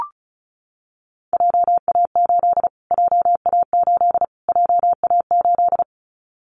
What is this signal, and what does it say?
Morse code, heard as a keyed tone, about 34 wpm, 700 Hz. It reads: JA8 JA8 JA8
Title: Callsign Prefixes for Top 25 DXCC - Repeated 3x - 35wpm